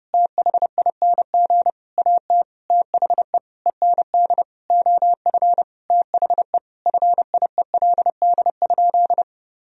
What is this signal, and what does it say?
Morse code, heard as a keyed tone, about 30 wpm, 700 Hz. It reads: THING AT THE END OF THE FIELD?
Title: Sentences from Top 1000 Words - Review Part 2 - 30wpm